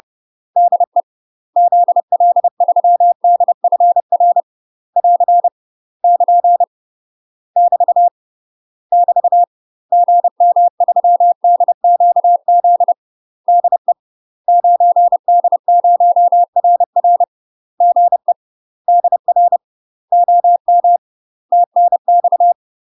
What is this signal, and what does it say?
Morse code, heard as a keyed tone, about 30 wpm, 700 Hz. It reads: DE ZL3DFR <AR> <KN> <BT> <BT> GM3DQZ DE 9D0RR GE DR OM TNX
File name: UK MASTER QSOs TRAINING / 30WPM / UK MASTER QSOs 0019